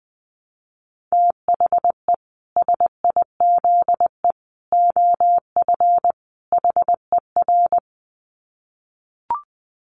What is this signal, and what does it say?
Morse code, heard as a keyed tone, about 20 wpm, 700 Hz. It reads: THE SIZE OF HER
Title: Dit Challenge 20wpm